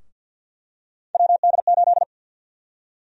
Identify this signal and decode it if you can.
Morse code, heard as a keyed tone, about 48 wpm, 700 Hz. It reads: WD8
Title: USA Call Sign Prefixes - Repeated 2x - 50wpm